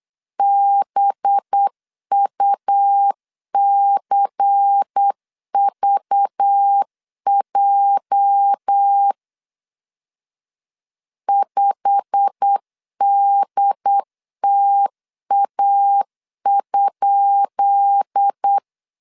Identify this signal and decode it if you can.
Morse code, heard as a keyed tone, about 8 wpm, 800 Hz. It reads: BUCVJ 5DTA?
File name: LEZIONI / 10 SIGNS / 10B INT 8wpm 3534